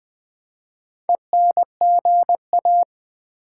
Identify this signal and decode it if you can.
Morse code, heard as a keyed tone, about 20 wpm, 700 Hz. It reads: ENGA
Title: Build A Word - 4 Letters At A Time - 8-Letter Words - 20wpm